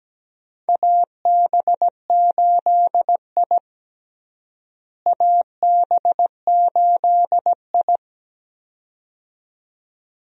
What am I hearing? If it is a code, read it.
Morse code, heard as a keyed tone, about 17 wpm, 700 Hz. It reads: AB8I AB8I
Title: USA Call Signs - 4 Characters - Repeated 2x - 17wpm